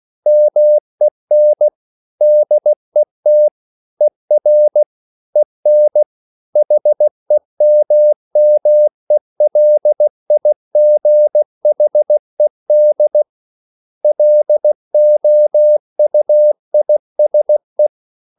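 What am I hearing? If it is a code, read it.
Morse code, heard as a keyed tone, about 16 wpm, 600 Hz. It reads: MEN DET ER EN HEMMELIGHED LOUISE
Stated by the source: Slow (QRS) Morse code at 16 wpm, from CW med Gnister.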